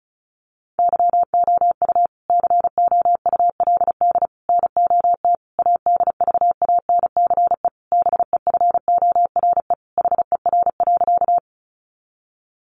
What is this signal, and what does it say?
Morse code, heard as a keyed tone, about 34 wpm, 700 Hz. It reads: YOU COULD NOT ADVANCE BEFORE HER.
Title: Sentences from Top 1400 Words 35wpm